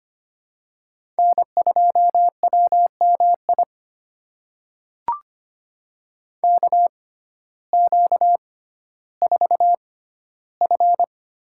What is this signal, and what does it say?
Morse code, heard as a keyed tone, about 25 wpm, spaced as for 14 wpm, 700 Hz. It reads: N2WMI KQ4F
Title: USA Call Signs - 5 Characters - 6x Character Spacing - 25wpm